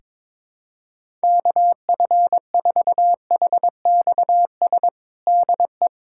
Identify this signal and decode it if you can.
Morse code, heard as a keyed tone, about 22 wpm, 700 Hz. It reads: KF4HXS DE
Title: USA Callsign DE Callsign - 6 Characters - 22wpm